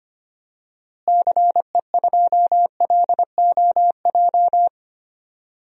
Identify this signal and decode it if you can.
Morse code, heard as a keyed tone, about 25 wpm, 700 Hz. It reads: CE2LOJ
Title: DXCC-100 Call Signs - 6 Characters - Repeated 3x - 25wpm